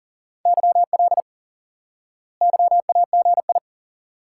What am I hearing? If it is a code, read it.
Morse code, heard as a keyed tone, about 40 wpm, 700 Hz. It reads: YL YAGI
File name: IQE Course 3 - Lesson 89 - YAGI - Two Elements - 40wpm